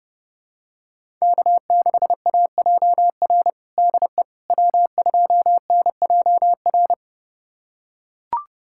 Morse code, heard as a keyed tone, about 30 wpm, 700 Hz. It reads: K6AJR DE W2NJR
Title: USA Callsign DE Callsign - 5 Characters - 30wpm